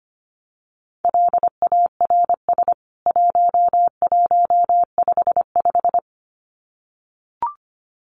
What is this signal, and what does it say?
Morse code, heard as a keyed tone, about 25 wpm, 700 Hz. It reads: LARS 1155
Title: CWT Exchange 25wpm